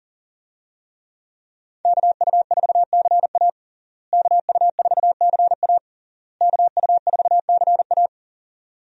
Morse code, heard as a keyed tone, about 40 wpm, 700 Hz. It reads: KU4CA KU4CA KU4CA
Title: USA Call Signs - 5 Characters - Repeated 3x - 40wpm